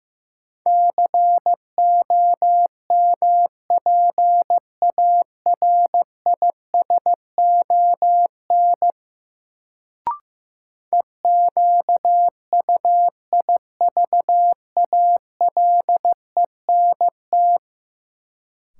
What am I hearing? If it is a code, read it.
Morse code, heard as a keyed tone, about 15 wpm, 700 Hz. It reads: COMPARISON EQUIVALENT
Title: Sentences from Top 1600 Words - New Word Drill Alt - 15wpm